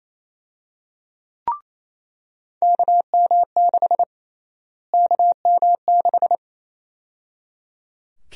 Morse code, heard as a keyed tone, about 28 wpm, 700 Hz. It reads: KM6 KM6
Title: USA Call Sign Prefixes - Repeated 2x - 28wpm